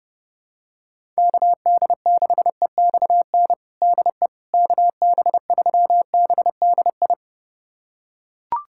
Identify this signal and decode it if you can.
Morse code, heard as a keyed tone, about 30 wpm, 700 Hz. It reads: KD6EXN DE KB3BDI
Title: USA Callsign DE Callsign 30wpm